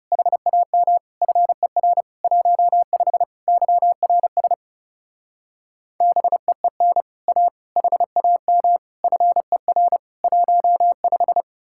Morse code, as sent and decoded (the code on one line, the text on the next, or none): .... .- -- / ..-. .-. / .---- ..... / -.-- .-. ... / -... . -. / .- / .... .- -- / ..-. .-. / .---- .....
HAM FER 15 YRS BEEN A HAM FER 15